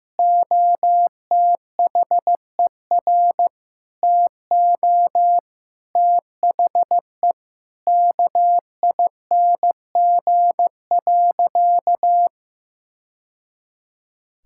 Morse code, heard as a keyed tone, about 16 wpm, 700 Hz. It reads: OTHER TO THE KING.
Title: Sentences from Top 1000 Words - Review Part 6 - 15wpm